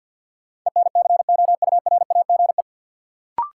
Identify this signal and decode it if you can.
Morse code, heard as a keyed tone, about 52 wpm, 700 Hz. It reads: ENCOURAGE
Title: Sentences from Top 1400 Words - New Word Drill Alt - 50wpm